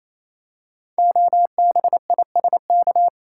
Morse code, heard as a keyed tone, about 27 wpm, 700 Hz. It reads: OBISK
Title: Code Groups - 5 Letters - 28wpm